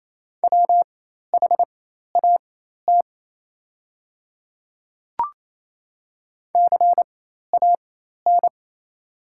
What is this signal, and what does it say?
Morse code, heard as a keyed tone, about 28 wpm, spaced as for 13 wpm, 700 Hz. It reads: WHAT CAN
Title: 4x Character Spacing - Top 100 Words - 28wpm